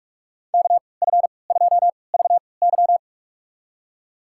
Morse code, heard as a keyed tone, about 46 wpm, spaced as for 25 wpm, 700 Hz. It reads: KF2VY
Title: USA Call Signs - 5 Characters - 3x Character Spacing - 45wpm